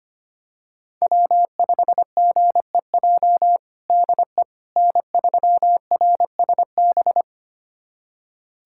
Morse code, heard as a keyed tone, about 25 wpm, 700 Hz. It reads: W5GEJ DE N3RSB
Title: USA Callsign DE Callsign - 5 Characters - 25wpm